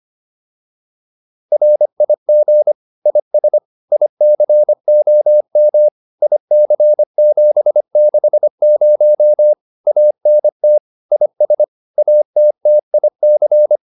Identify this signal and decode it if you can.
Morse code, heard as a keyed tone, about 25 wpm, 600 Hz. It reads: RIG IS ICOM IC760 ANT IS ATTIC